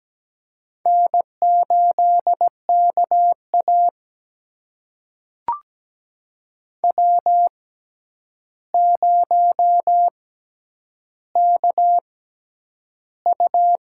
Morse code, heard as a keyed tone, about 17 wpm, spaced as for 9 wpm, 700 Hz. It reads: N8KA W0KU
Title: USA Call Signs - 4 Characters - 6x Character Spacing - 17wpm